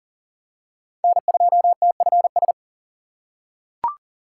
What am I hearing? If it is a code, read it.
Morse code, heard as a keyed tone, about 40 wpm, 700 Hz. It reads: N2TFS
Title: USA Call Signs - 5 Characters - 4x Character Spacing - 40wpm